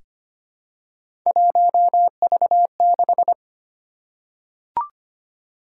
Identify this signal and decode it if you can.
Morse code, heard as a keyed tone, about 25 wpm, 700 Hz. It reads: 1V6
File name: Three Letter-Number Combinations - 25wpm